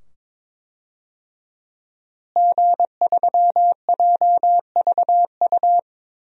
Morse code, heard as a keyed tone, about 22 wpm, 700 Hz. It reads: G3JVU
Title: DXCC-100 Call Signs - 5 Characters - Repeated 2x - 22wpm